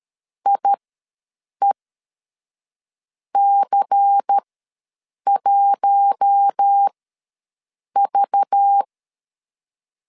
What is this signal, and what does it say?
Morse code, heard as a keyed tone, about 13 wpm, spaced as for 6 wpm, 800 Hz. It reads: IEC1V
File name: LEZIONI / 5 LBVC / 5 LBVC ws 12wpm 3159